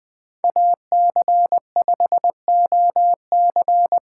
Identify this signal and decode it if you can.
Morse code, heard as a keyed tone, about 20 wpm, 700 Hz. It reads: AC5OC
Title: 2023 POTA Hunter Call Signs - 20wpm